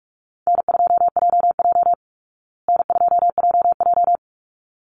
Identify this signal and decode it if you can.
Morse code, heard as a keyed tone, about 44 wpm, 700 Hz. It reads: N2JJ N2JJ
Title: Call Signs Repeated 3x - 45wpm